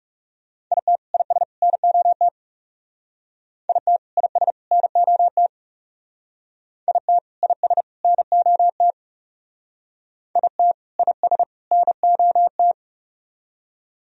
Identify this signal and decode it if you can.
Morse code, heard as a keyed tone, about 37 wpm, 700 Hz. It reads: IT IS NOT IT IS NOT IT IS NOT IT IS NOT
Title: Speed-Racing - Sets of 3 Words 30wpm